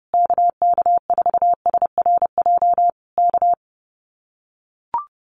Morse code, heard as a keyed tone, about 30 wpm, 700 Hz. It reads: KK4SRJ K
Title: Common QSO Phrases Encore 30wpm